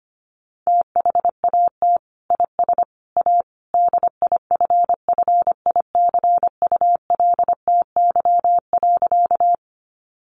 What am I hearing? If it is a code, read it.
Morse code, heard as a keyed tone, about 25 wpm, 700 Hz. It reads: THAT IS A DIFFICULTY.
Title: Sentences from Top 1500 Words 25wpm